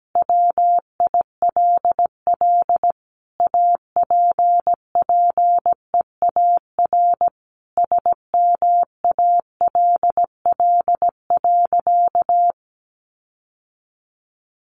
Morse code, heard as a keyed tone, about 17 wpm, 700 Hz. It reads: WILL APPEAR SMALL.
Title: Sentences from Top 1500 Words 17wpm